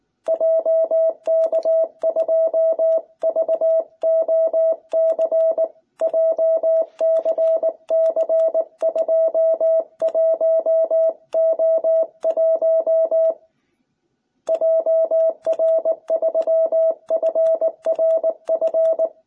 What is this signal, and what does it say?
Morse code, heard as a keyed tone, about 19 wpm, 660 Hz. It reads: JK2VOC JCC21O1 JR3FRF